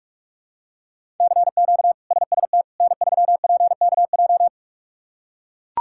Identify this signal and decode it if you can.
Morse code, heard as a keyed tone, about 41 wpm, 700 Hz. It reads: CQ SST N3PKJ
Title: K1USN SST QSO Practice - 45wpm